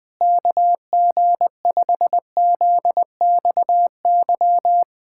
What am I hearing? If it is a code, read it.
Morse code, heard as a keyed tone, about 20 wpm, 700 Hz. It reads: KG5ZXY
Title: USA Callsign DE Callsign 20wpm